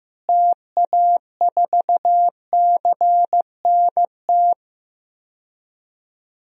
Morse code, heard as a keyed tone, about 15 wpm, 700 Hz. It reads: TA4CNT
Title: DXCC-100 Call Signs - 6 Characters - Repeated 2x - 15wpm